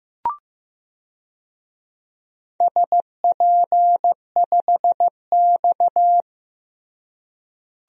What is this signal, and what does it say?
Morse code, heard as a keyed tone, about 15 wpm, 700 Hz. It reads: SP5X
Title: DXCC-100 Call Signs - 4 Characters - Repeated 2x - 15wpm